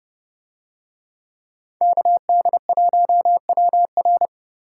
Morse code, heard as a keyed tone, about 29 wpm, 700 Hz. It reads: KD1WR